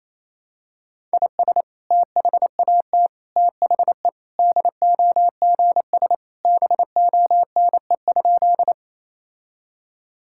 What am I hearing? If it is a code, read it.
Morse code, heard as a keyed tone, about 28 wpm, 700 Hz. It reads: IS THAT THE DOGS BONE?
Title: Sentences from Top 1600 Words 28wpm